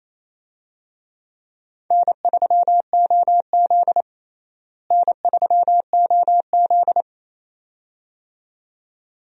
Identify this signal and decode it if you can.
Morse code, heard as a keyed tone, about 28 wpm, 700 Hz. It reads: N3OZ N3OZ
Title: USA Call Signs - 4 Characters - Repeated 2x - 28wpm